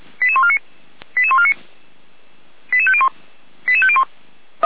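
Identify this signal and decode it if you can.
Five-tone selective call (ZVEI), heard as quick runs of five tones, tones of about 70 ms. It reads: 88149 88149 88501 88501
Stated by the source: ZVEI five-tone selective call received by SDR and recorded by the fire brigade (Feuerwehr) Orlamünde.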